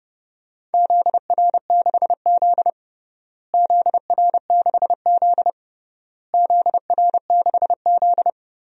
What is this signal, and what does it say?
Morse code, heard as a keyed tone, about 30 wpm, 700 Hz. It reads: ZR6Z ZR6Z ZR6Z
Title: DXCC-100 Call Signs - 4 Characters - Repeated 3x - 30wpm